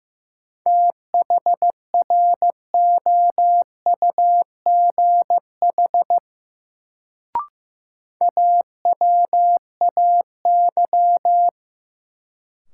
Morse code, heard as a keyed tone, about 15 wpm, 700 Hz. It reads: THROUGH AWAY
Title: Sets of 2 Words - New Word Drill - 15wpm